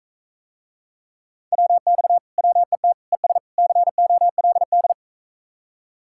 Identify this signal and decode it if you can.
Morse code, heard as a keyed tone, about 45 wpm, 700 Hz. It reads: WX WET ES COLD